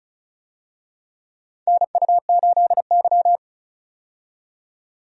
Morse code, heard as a keyed tone, about 34 wpm, 700 Hz. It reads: NU8Y
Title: USA Call Signs - 4 Characters - 35wpm